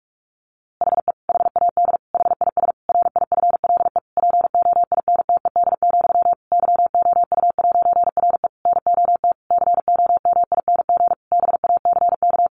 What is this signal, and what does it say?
Morse code, heard as a keyed tone, about 44 wpm, 700 Hz. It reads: HE HAD HIS RIFLE POINTED, YOU'RE NOT COMING BACK
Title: A Sound Of Thunder 45wpm